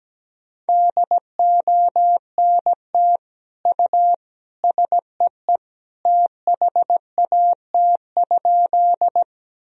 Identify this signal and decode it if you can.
Morse code, heard as a keyed tone, about 17 wpm, 700 Hz. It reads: DONT U SEE THAT?
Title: Sentences from Top 300 Words and Abbreviations 17wpm